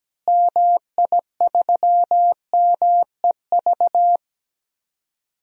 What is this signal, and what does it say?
Morse code, heard as a keyed tone, about 17 wpm, 700 Hz. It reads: MI3MEV
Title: DXCC-100 Call Signs - 6 Characters - Repeated 2x - 17wpm